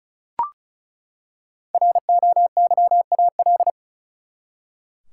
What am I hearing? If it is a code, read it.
Morse code, heard as a keyed tone, about 36 wpm, 700 Hz. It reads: ROYAL
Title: Five Letter Words - 35wpm